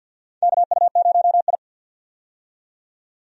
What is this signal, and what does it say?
Morse code, heard as a keyed tone, about 47 wpm, 700 Hz. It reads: KU0I
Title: USA Call Signs - 4 Characters - 50wpm